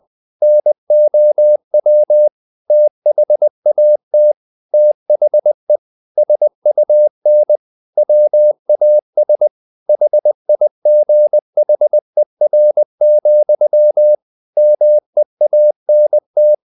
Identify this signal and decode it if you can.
Morse code, heard as a keyed tone, about 20 wpm, 600 Hz. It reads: NOW THAT THE SUN WAS HIGHER, MEANT
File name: the old man and the sea / the old man and the sea part 1 20wpm